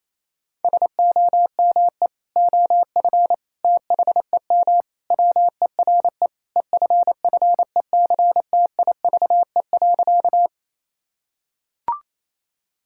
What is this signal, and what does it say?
Morse code, heard as a keyed tone, about 28 wpm, 700 Hz. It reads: SOME OF THEM WERE EFFECTIVE.
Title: Sentences from Top 1000 Words - Review Part 6 - 28wpm